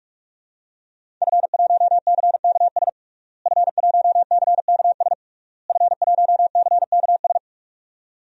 Morse code, heard as a keyed tone, about 46 wpm, 700 Hz. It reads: F1CKS F1CKS F1CKS
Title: DXCC-100 Call Signs - 5 Characters - Repeated 3x - 45wpm